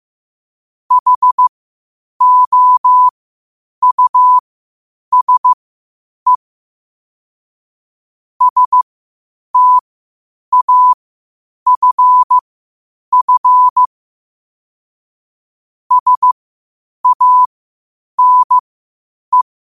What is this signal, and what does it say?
Morse code, heard as a keyed tone, about 15 wpm, spaced as for 8 wpm, 1000 Hz. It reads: HOUSE STAFF SANE
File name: qotdfiles / 8wpm